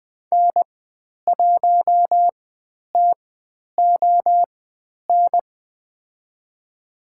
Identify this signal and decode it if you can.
Morse code, heard as a keyed tone, about 20 wpm, spaced as for 10 wpm, 700 Hz. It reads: N1TON
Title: 095 Lesson 16g - Letter c - Call Signs 20wpm